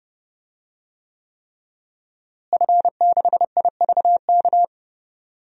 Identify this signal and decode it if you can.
Morse code, heard as a keyed tone, about 30 wpm, 700 Hz. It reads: F6IVK
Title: DXCC-100 Call Signs - 5 Characters - Repeated 3x - 30wpm